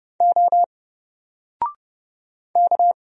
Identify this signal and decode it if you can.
Morse code, heard as a keyed tone, about 30 wpm, 700 Hz. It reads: O K
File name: Single-Letter-Number-30wpm